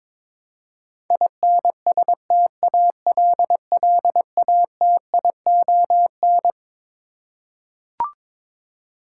Morse code, heard as a keyed tone, about 21 wpm, 700 Hz. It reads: INSTALLATION
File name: Twelve Letter Words - 22wpm